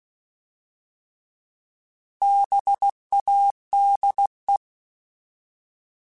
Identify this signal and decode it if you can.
Morse code, heard as a keyed tone, about 16 wpm, 800 Hz. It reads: BADE
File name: Alphabetical / Alphabetical mp3s / Bs @15